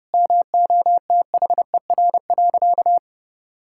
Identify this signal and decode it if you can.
Morse code, heard as a keyed tone, about 30 wpm, 700 Hz. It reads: MOTHER.
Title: Sentences from Top 1600 Words 30wpm